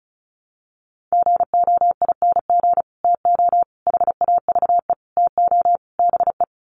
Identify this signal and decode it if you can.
Morse code, heard as a keyed tone, about 34 wpm, 700 Hz. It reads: GOING TO HAVE TO BE
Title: Sets of 5 Words 35wpm